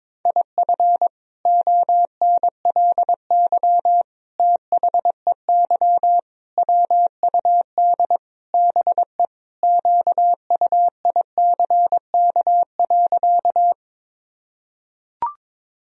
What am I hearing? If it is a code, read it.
Morse code, heard as a keyed tone, about 22 wpm, 700 Hz. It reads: IF ONLY THEY WUD BE QUICK.
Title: Sentences from Top 500 Words and Abbreviations 22wpm